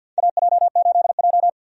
Morse code, heard as a keyed tone, about 47 wpm, 700 Hz. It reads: AJ8J
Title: USA Call Signs - 4 Characters - 50wpm